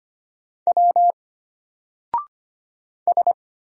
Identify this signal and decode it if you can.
Morse code, heard as a keyed tone, about 25 wpm, 700 Hz. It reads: W S
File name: Single-Letter-Number-25wpm